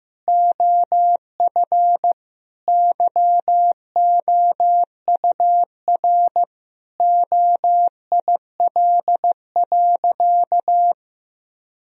Morse code, heard as a keyed tone, about 16 wpm, 700 Hz. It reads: OF YOUR OIL.
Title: Sentences from Top 1000 Words - Review Part 4 - 15wpm